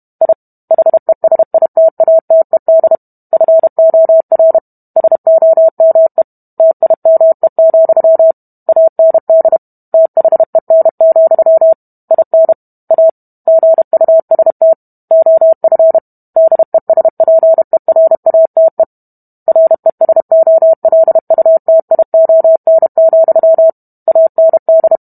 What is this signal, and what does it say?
Morse code, heard as a keyed tone, about 32 wpm, 650 Hz. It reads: I HESITATED FOR SOME TIME, AND THEN, IN A GUST OF DESPERATE RESOLUTION, AND